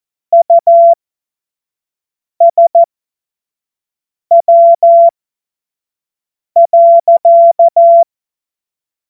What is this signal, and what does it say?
Morse code, heard as a keyed tone, about 14 wpm, spaced as for 5 wpm, 680 Hz. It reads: USW.
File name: Morsekurs / Witze / Witz 15. August 2023 25 F70